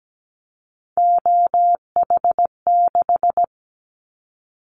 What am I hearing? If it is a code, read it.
Morse code, heard as a keyed tone, about 17 wpm, 700 Hz. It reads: OH6
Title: Callsign Prefixes for Top 50 DXCC - 17wpm